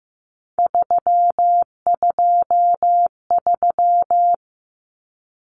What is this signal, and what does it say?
Morse code, heard as a keyed tone, about 15 wpm, 700 Hz. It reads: 323
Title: Common Cut Numbers - 3-Digit 15wpm